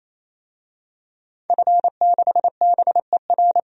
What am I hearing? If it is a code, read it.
Morse code, heard as a keyed tone, about 28 wpm, 700 Hz. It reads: F6BER
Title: DXCC-100 Call Signs - 5 Characters - 28wpm